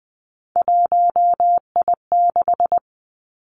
Morse code, heard as a keyed tone, about 20 wpm, 700 Hz. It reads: 1I6
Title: Three Challenging Letter-Number Combinations 20wpm